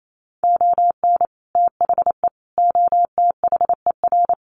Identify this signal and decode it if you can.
Morse code, heard as a keyed tone, about 28 wpm, 700 Hz. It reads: ON THE OTHER
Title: Sets of 5 Words Encore 28wpm